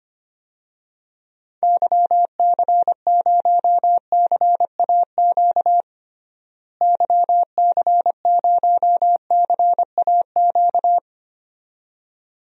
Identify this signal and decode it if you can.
Morse code, heard as a keyed tone, about 25 wpm, 700 Hz. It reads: YC0CAQ YC0CAQ
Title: DXCC-100 Call Signs - 6 Characters - Repeated 2x - 25wpm